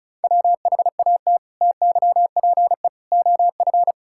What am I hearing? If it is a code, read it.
Morse code, heard as a keyed tone, about 34 wpm, 700 Hz. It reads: WHAT TYPE OF
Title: Sentences from Top 1000 Words - Review Part 6 - 35wpm